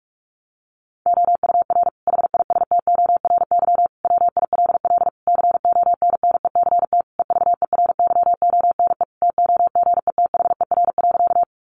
Morse code, heard as a keyed tone, about 44 wpm, 700 Hz. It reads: OUR HISTORY WILL CONNECT EVERYONE TOGETHER.
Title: Sentences from Top 1500 Words 45wpm